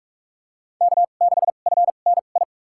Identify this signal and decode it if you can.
Morse code, heard as a keyed tone, about 43 wpm, spaced as for 32 wpm, 700 Hz. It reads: K6FNI
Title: USA Call Signs - 5 Characters - 2x Character Spacing - 45wpm